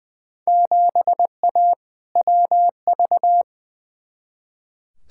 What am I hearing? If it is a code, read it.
Morse code, heard as a keyed tone, about 20 wpm, 700 Hz. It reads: 7A WV
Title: Field Day Exchanges Encore - 20wpm